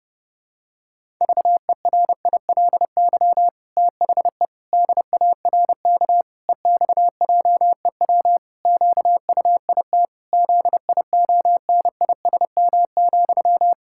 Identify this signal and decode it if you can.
Morse code, heard as a keyed tone, about 32 wpm, 700 Hz. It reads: VERILY THE DARK EXJEW QUIT ZIONISM,
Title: Pangram Challenge v2 - 30wpm